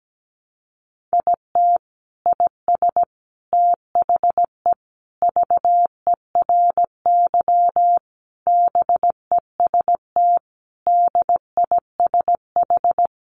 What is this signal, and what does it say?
Morse code, heard as a keyed tone, about 17 wpm, 700 Hz. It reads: IT IS THE VERY BEST DISH
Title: Sentences from Top 1300 Words 17wpm